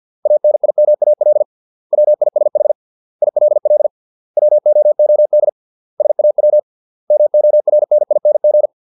Morse code, heard as a keyed tone, about 50 wpm, 600 Hz. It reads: ANIMAL WISH ILL WOOD SAW MORNING